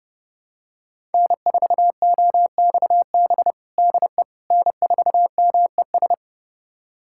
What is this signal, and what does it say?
Morse code, heard as a keyed tone, about 30 wpm, 700 Hz. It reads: N4OXB DE N4MES
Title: USA Callsign DE Callsign - 5 Characters - 30wpm